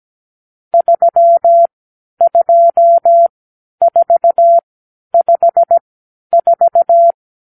Morse code, heard as a keyed tone, about 17 wpm, spaced as for 10 wpm, 680 Hz. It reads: 32454
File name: arrl / Review (1-5)